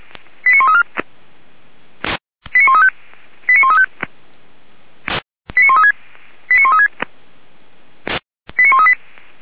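Five-tone selective call (ZVEI), heard as quick runs of five tones, tones of about 70 ms. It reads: 89134 89135 89135 89136 89136 89138
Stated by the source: ZVEI five-tone call received via SDR, from the Feuerwehr Orlamünde audio server.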